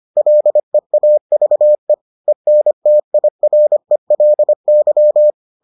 Morse code, heard as a keyed tone, about 25 wpm, 600 Hz. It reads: LEAVE ENTIRELY